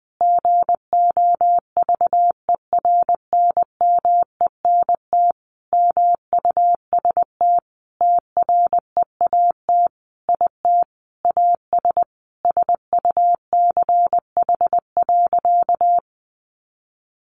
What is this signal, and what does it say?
Morse code, heard as a keyed tone, about 20 wpm, 700 Hz. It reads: GOVERNMENT MUST TREAT IT AS SUCH.
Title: Sentences from Top 1300 Words 20wpm